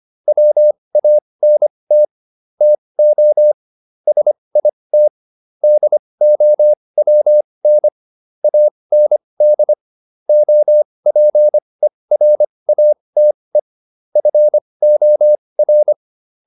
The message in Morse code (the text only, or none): WANT TO SIT DOWN AND OPERATE FOR